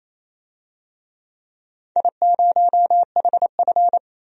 Morse code, heard as a keyed tone, about 28 wpm, 700 Hz. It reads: I0HF
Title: DXCC-100 Call Signs - 4 Characters - 28wpm